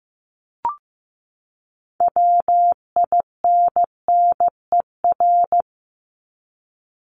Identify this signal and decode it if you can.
Morse code, heard as a keyed tone, about 15 wpm, 700 Hz. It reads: WINNER
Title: Top 1751-2000 Words 15wpm